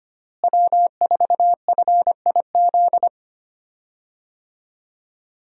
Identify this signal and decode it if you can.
Morse code, heard as a keyed tone, about 25 wpm, 700 Hz. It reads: W4FIZ
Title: USA Call Signs - 5 Characters - Repeated 2x - 25wpm